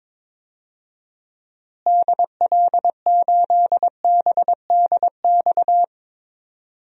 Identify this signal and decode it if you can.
Morse code, heard as a keyed tone, about 22 wpm, 700 Hz. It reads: DL8BDX
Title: DXCC-100 Call Signs - 6 Characters - Repeated 3x - 22wpm